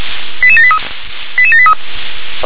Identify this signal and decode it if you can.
Five-tone selective call (ZVEI), heard as quick runs of five tones, tones of about 70 ms. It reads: 88783 88783